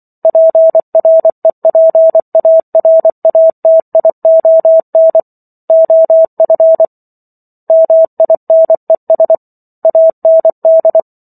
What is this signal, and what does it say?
Morse code, heard as a keyed tone, about 24 wpm, 650 Hz. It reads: PREPARATION OF MINES AND
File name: Chapter 15 - 24 WPM